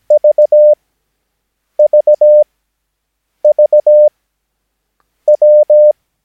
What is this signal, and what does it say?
Morse code, heard as a keyed tone, about 17 wpm, spaced as for 7 wpm, 600 Hz. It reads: VVVW
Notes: Övningsmaterial med alfabetet A till Ö samt siffror 0 till 9 enligt International/Continental Morse Code.